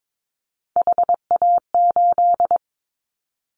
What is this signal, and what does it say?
Morse code, heard as a keyed tone, about 22 wpm, 700 Hz. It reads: HA8
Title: Callsign Prefixes for Top 25 DXCC - Repeated 3x - 22wpm